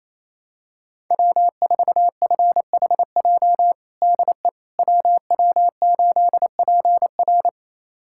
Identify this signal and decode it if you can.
Morse code, heard as a keyed tone, about 27 wpm, 700 Hz. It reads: W4FHJ DE WW8PR
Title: USA Callsign DE Callsign - 5 Characters - 28wpm